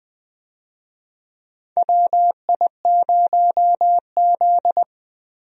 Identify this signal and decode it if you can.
Morse code, heard as a keyed tone, about 20 wpm, 700 Hz. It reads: WI0Z
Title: USA Call Signs - 4 Characters - Repeated 3x - 20wpm